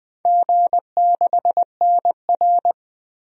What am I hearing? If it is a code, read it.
Morse code, heard as a keyed tone, about 20 wpm, 700 Hz. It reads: G6NR
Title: DXCC-100 Call Signs - 4 Characters - Repeated 3x - 20wpm